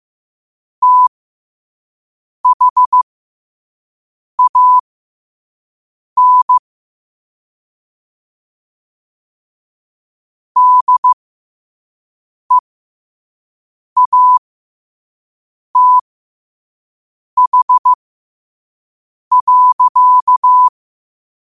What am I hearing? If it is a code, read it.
Morse code, heard as a keyed tone, about 15 wpm, spaced as for 5 wpm, 1000 Hz. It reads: THAN DEATH.
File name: qotdfiles / 5wpm